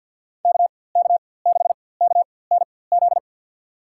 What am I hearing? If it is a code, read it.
Morse code, heard as a keyed tone, about 52 wpm, spaced as for 23 wpm, 700 Hz. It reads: KK6KNZ
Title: USA Call Signs - 6 Characters - 4x Character Spacing - 50wpm